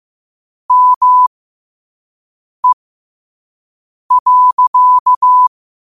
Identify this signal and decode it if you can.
Morse code, heard as a keyed tone, about 15 wpm, spaced as for 5 wpm, 1000 Hz. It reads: ME.
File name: qotdfiles / 5wpm